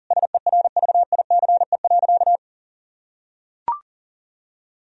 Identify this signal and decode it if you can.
Morse code, heard as a keyed tone, about 40 wpm, 700 Hz. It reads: SERVICE.
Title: Sentences from Top 1000 Words - Review Part 4 - 40wpm